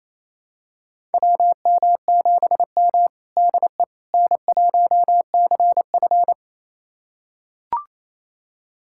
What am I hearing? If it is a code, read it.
Morse code, heard as a keyed tone, about 28 wpm, 700 Hz. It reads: WM7M DE N1CF
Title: USA Callsign DE Callsign - 4 Characters - 28wpm